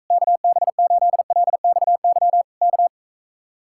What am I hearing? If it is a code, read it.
Morse code, heard as a keyed tone, about 44 wpm, 700 Hz. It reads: KB8LXY K